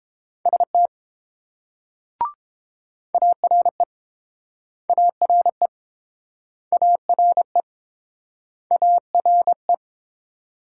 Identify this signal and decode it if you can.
Morse code, heard as a keyed tone, about 30 wpm, 700 Hz. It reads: ST ARE ARE ARE ARE